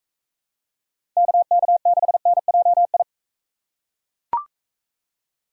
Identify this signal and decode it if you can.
Morse code, heard as a keyed tone, about 41 wpm, 700 Hz. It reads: KK6NJI